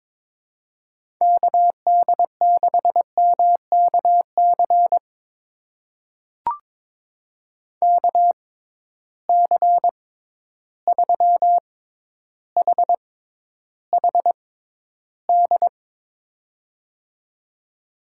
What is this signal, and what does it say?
Morse code, heard as a keyed tone, about 22 wpm, spaced as for 11 wpm, 700 Hz. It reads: KD6MKC KC3HHD
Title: USA Call Signs - 6 Characters - 6x Character Spacing - 22wpm